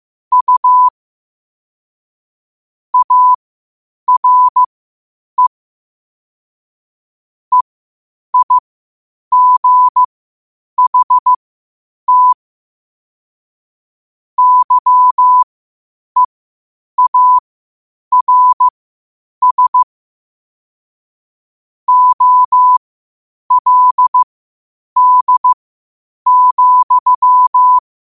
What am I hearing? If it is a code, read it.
Morse code, heard as a keyed tone, about 15 wpm, spaced as for 8 wpm, 1000 Hz. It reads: U ARE EIGHT YEARS OLD,
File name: qotdfiles / 8wpm